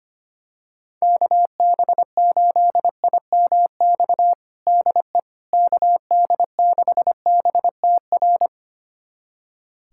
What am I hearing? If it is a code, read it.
Morse code, heard as a keyed tone, about 25 wpm, 700 Hz. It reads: KB8IMX DE KD6BTR
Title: USA Callsign DE Callsign 25wpm